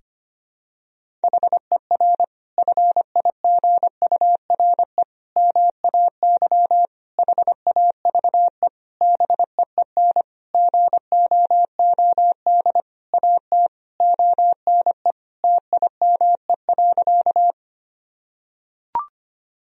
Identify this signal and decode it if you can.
Morse code, heard as a keyed tone, about 25 wpm, 700 Hz. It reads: HER FIGURE MAY HAVE BEEN GOOD AT ONE TIME.
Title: Sentences from Top 1000 Words - Review Part 2 - 25wpm